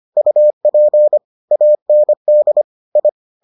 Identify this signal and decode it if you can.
Morse code, heard as a keyed tone, about 25 wpm, 600 Hz. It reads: UP AND I